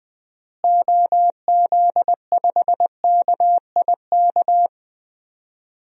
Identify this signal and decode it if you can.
Morse code, heard as a keyed tone, about 20 wpm, 700 Hz. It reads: OZ5KIK
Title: DXCC-100 Call Signs - 6 Characters - Repeated 2x - 20wpm